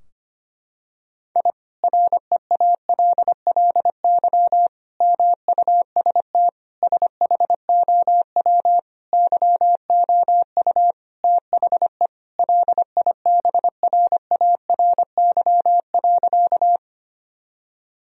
Morse code, heard as a keyed tone, about 25 wpm, 700 Hz. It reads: I REALLY MUST SHOW YOU THE LIBRARY.
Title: Sentences from Top 1000 Words - Review Part 4 - 25wpm